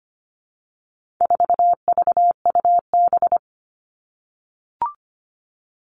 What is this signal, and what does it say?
Morse code, heard as a keyed tone, about 25 wpm, 700 Hz. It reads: 4VUB
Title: All Cut Numbers - 4-Digit 25wpm